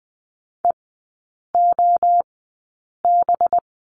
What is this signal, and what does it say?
Morse code, heard as a keyed tone, about 19 wpm, spaced as for 8 wpm, 700 Hz. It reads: EOB
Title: 184 Lesson 30e - Letter b - Three Characters 20wpm